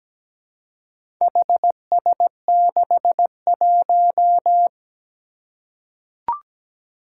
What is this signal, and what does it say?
Morse code, heard as a keyed tone, about 17 wpm, 700 Hz. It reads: HS61
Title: Four Challenging Letter-Number Combinations 17wpm